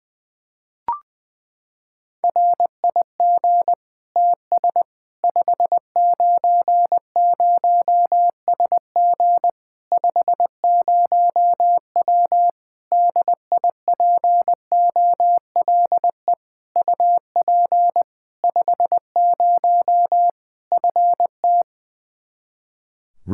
Morse code, heard as a keyed tone, about 20 wpm, 700 Hz. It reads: RIG TS 590SG 50W DIPOLE UP 50 FT
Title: Practice QSO - Exchange 2 - Info - 20wpm